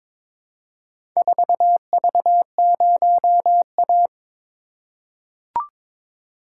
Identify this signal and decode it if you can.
Morse code, heard as a keyed tone, about 22 wpm, 700 Hz. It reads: 4V0A
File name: practice / 22wpm / Four Letter-Number Combinations - 22wpm